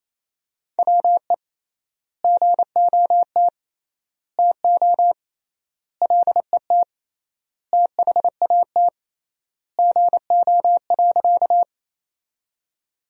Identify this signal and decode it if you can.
Morse code, heard as a keyed tone, about 28 wpm, 700 Hz. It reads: WE GOT TO LET THAT GO.
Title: 3x Word-Spacing - Sentences from Top 300 Words - 28wpm